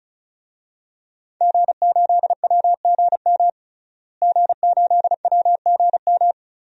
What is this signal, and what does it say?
Morse code, heard as a keyed tone, about 36 wpm, 700 Hz. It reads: G8WGM G8WGM
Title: DXCC-100 Call Signs - 5 Characters - Repeated 3x - 35wpm